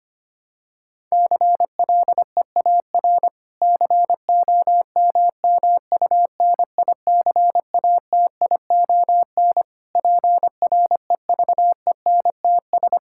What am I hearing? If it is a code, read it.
Morse code, heard as a keyed tone, about 25 wpm, 700 Hz. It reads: CLEAR COMMUNICATION PREVENTS
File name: practice / 25wpm / Difficult Sentences with Hints - 25wpm